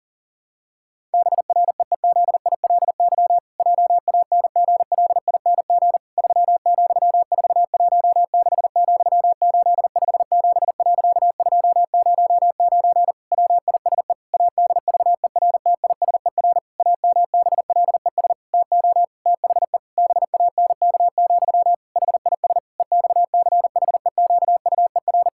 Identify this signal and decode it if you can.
Morse code, heard as a keyed tone, about 40 wpm, 700 Hz. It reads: BREEZILY JANGLING 3,416,857.209 WISE ADVERTISER AMBLES TO THE BANK, HIS EXCHEQUER
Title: Pangram Challenge v2 - 40wpm